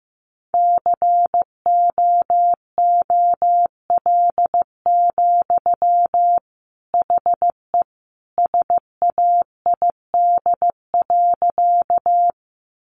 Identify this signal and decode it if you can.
Morse code, heard as a keyed tone, about 15 wpm, 700 Hz. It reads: COOL, HE SAID.
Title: Sentences from Top 1200 Words 15wpm